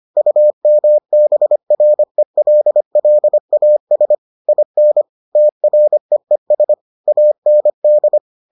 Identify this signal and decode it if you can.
Morse code, heard as a keyed tone, about 25 wpm, 600 Hz. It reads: UMBRELLAS IN TREES AND